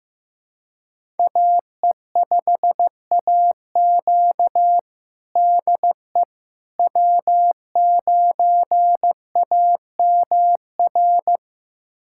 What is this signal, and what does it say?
Morse code, heard as a keyed tone, about 15 wpm, 700 Hz. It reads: AE5AQ DE W9AMR
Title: USA Callsign DE Callsign - 5 Characters - 15wpm